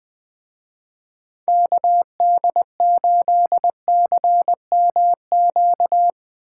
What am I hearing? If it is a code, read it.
Morse code, heard as a keyed tone, about 20 wpm, 700 Hz. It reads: KD8CMQ
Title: Speed Racing - USA 2x3 Call Signs 17wpm